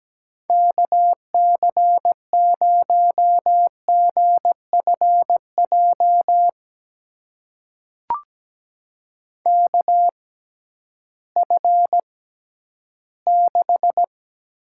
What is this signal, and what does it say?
Morse code, heard as a keyed tone, about 17 wpm, spaced as for 11 wpm, 700 Hz. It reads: KC0GFJ KF6
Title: USA Call Signs - 6 Characters - 6x Character Spacing - 17wpm